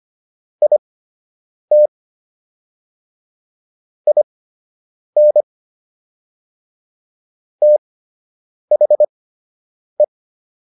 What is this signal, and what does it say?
Morse code, heard as a keyed tone, about 25 wpm, spaced as for 8 wpm, 600 Hz. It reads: IT IN THE